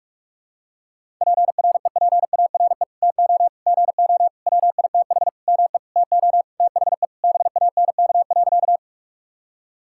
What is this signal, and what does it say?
Morse code, heard as a keyed tone, about 46 wpm, 700 Hz. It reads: PREPARE TO GO WITH ME TO THE BANK.
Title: Sentences from Top 1000 Words - Review Part 6 - 45wpm